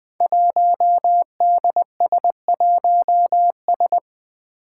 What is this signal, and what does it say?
Morse code, heard as a keyed tone, about 20 wpm, 700 Hz. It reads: 1DS1S
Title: Five Challenging Letter-Number Combinations 20wpm